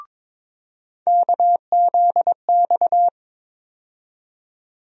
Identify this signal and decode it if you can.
Morse code, heard as a keyed tone, about 22 wpm, 700 Hz. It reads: KZX
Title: USA Call Sign Postfixes - 3 Letters 22wpm